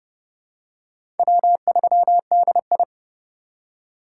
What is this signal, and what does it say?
Morse code, heard as a keyed tone, about 30 wpm, 700 Hz. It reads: W3DI
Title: USA Call Signs - 4 Characters - 6x Character Spacing - 30wpm